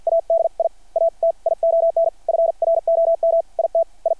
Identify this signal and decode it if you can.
Morse code, heard as a keyed tone, about 53 wpm, spaced as for 41 wpm, 660 Hz. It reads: ADIATION FROM IT I
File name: archived / CRSnet